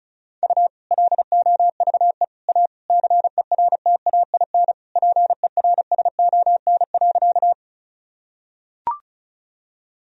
Morse code, heard as a keyed tone, about 34 wpm, 700 Hz. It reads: U LOVE A CERTAIN PERSON.
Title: Sentences from Top 500 Words and Abbreviations 35wpm